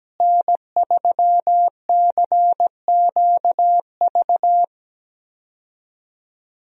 Morse code, heard as a keyed tone, about 17 wpm, 700 Hz. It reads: N3CQV
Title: USA Call Signs - 5 Characters - Repeated 2x - 17wpm